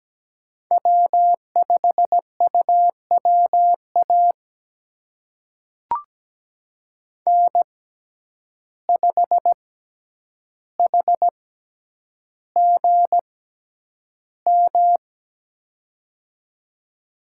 Morse code, heard as a keyed tone, about 17 wpm, spaced as for 9 wpm, 700 Hz. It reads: W5UWA N5HGM
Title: USA Call Signs - 5 Characters - 6x Character Spacing - 17wpm